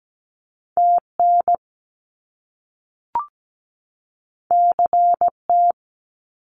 Morse code, heard as a keyed tone, about 17 wpm, 700 Hz. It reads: TN CT
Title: US States 17wpm